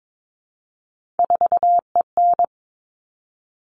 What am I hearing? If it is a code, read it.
Morse code, heard as a keyed tone, about 22 wpm, 700 Hz. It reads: 4EN
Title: All Cut Numbers - 3-Digit 22wpm